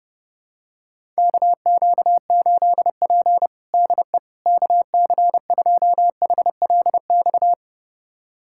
Morse code, heard as a keyed tone, about 29 wpm, 700 Hz. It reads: KQ8P DE KC2HLX
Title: USA Callsign DE Callsign 30wpm